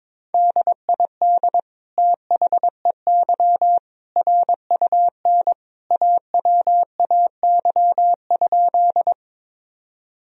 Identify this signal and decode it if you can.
Morse code, heard as a keyed tone, about 22 wpm, 700 Hz. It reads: DID THEY RUN AWAY?
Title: Sentences from Top 1000 Words - Review Part 1 - 22wpm